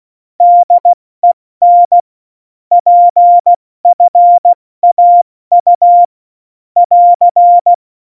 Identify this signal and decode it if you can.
Morse code, heard as a keyed tone, about 16 wpm, 700 Hz. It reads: DEN PFAU <AR>
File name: Morsekurs / Witze / Witz 28 April 2020, 70 F80